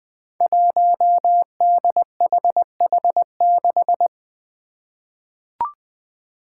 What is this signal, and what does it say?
Morse code, heard as a keyed tone, about 20 wpm, 700 Hz. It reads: 1DHH6
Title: Five Challenging Letter-Number Combinations 20wpm